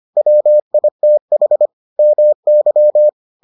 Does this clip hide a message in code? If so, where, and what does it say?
Yes: Morse code from 0.2 to 3.1 s, heard as a keyed tone, about 25 wpm, 600 Hz. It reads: WITH MY